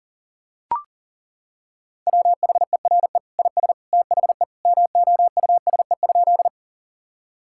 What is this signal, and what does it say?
Morse code, heard as a keyed tone, about 40 wpm, 700 Hz. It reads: WHERE IS THE MOUSE?
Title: Sentences from Top 1000 Words - Review Part 5 - 40wpm